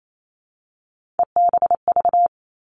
Morse code, heard as a keyed tone, about 28 wpm, 700 Hz. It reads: EBV
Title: Three Challenging Letter-Number Combinations 28wpm